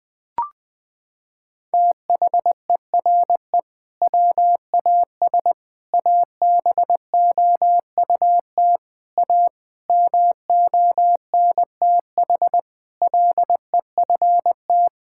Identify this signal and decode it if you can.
Morse code, heard as a keyed tone, about 20 wpm, 700 Hz. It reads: THERE WAS ABOUT A MONTH LEFT
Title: Sentences from Top 1000 Words - Review Part 5 - 20wpm